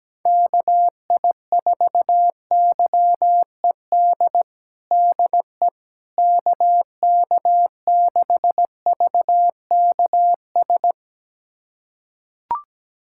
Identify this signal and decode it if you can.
Morse code, heard as a keyed tone, about 17 wpm, 700 Hz. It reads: KI4YED DE KK6VKS
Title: USA Callsign DE Callsign - 6 Characters - 17wpm